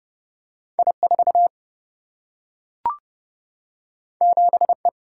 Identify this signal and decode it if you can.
Morse code, heard as a keyed tone, about 30 wpm, 700 Hz. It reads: I4 7E
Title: Single-Letter Single-Number 30wpm